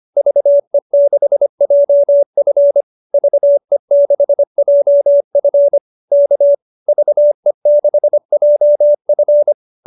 Morse code, heard as a keyed tone, about 25 wpm, 590 Hz. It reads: VE6JF VE6JF K VE6JF
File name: QSO303 25